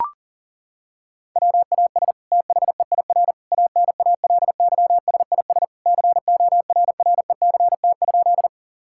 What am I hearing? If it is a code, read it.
Morse code, heard as a keyed tone, about 40 wpm, 700 Hz. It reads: WAS THEIR ANALYSIS CORRECT?
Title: Sentences from Top 1000 Words - Review Part 5 - 40wpm